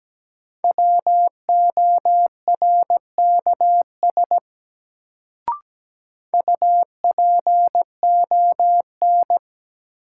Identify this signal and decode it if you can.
Morse code, heard as a keyed tone, about 17 wpm, 700 Hz. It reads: WORKS UPON
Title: Sentences from Top 1100 Words - New Word Drill Alt - 17wpm